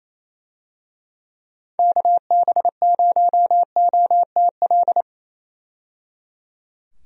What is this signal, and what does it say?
Morse code, heard as a keyed tone, about 27 wpm, 700 Hz. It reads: KB0OTL
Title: USA Call Signs - 6 Characters - 28wpm